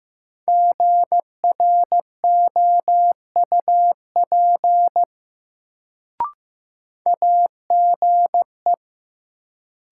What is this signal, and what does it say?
Morse code, heard as a keyed tone, about 15 wpm, 700 Hz. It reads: GROUP AGE
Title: Sets of 3 Words - New Word Drill - 15wpm